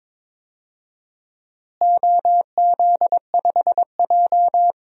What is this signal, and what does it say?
Morse code, heard as a keyed tone, about 22 wpm, 700 Hz. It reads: OZ5J
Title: DXCC-100 Call Signs - 4 Characters - Repeated 2x - 22wpm